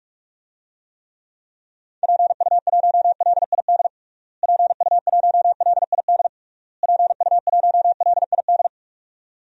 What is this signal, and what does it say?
Morse code, heard as a keyed tone, about 46 wpm, 700 Hz. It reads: PU1LID PU1LID PU1LID
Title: DXCC-100 Call Signs - 6 Characters - Repeated 3x - 45wpm